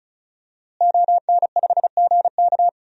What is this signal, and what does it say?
Morse code, heard as a keyed tone, about 34 wpm, 700 Hz. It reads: ON5GK
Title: DXCC-100 Call Signs - 5 Characters - Repeated 3x - 35wpm